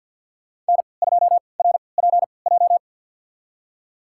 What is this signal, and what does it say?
Morse code, heard as a keyed tone, about 52 wpm, spaced as for 28 wpm, 700 Hz. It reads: N2RPJ
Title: USA Call Signs - 5 Characters - 3x Character Spacing - 50wpm